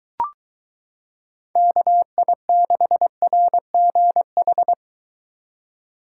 Morse code, heard as a keyed tone, about 23 wpm, 700 Hz. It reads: KI6RGH